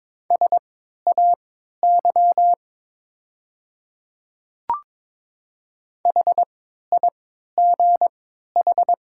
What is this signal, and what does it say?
Morse code, heard as a keyed tone, about 22 wpm, spaced as for 13 wpm, 690 Hz. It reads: SAY HIGH
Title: 3x Character Spacing - Top 200 Words - 22wpm